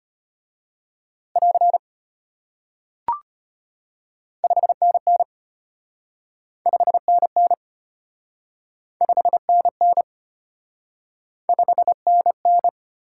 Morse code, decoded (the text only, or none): <AR> 5NN 5NN 5NN 5NN